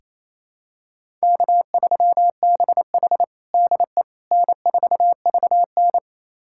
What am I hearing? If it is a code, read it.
Morse code, heard as a keyed tone, about 28 wpm, 700 Hz. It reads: K3BH DE N4VN
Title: USA Callsign DE Callsign - 4 Characters - 28wpm